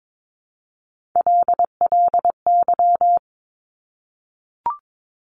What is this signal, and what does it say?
Morse code, heard as a keyed tone, about 22 wpm, 700 Hz. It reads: LLY
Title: Top Three Letter Combinations in Words 22wpm